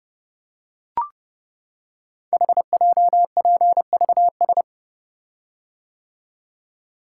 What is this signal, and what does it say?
Morse code, heard as a keyed tone, about 30 wpm, 700 Hz. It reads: HJPVS
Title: Code Groups - 5 Letters - 30wpm